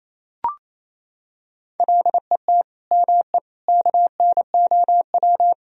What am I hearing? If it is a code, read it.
Morse code, heard as a keyed tone, about 28 wpm, 700 Hz. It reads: LET ME KNOW
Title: Sentences from Top 1000 Words - Review Part 2 - 28wpm